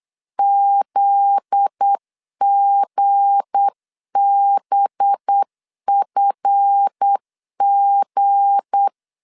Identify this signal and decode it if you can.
Morse code, heard as a keyed tone, about 8 wpm, 800 Hz. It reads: ZGBFG
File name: CW / DET / 1 9 1500C S A 8wpm 5637